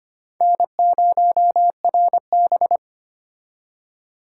Morse code, heard as a keyed tone, about 25 wpm, 700 Hz. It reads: N0RB
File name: practice / 25wpm / Four Letter-Number Combinations - 25wpm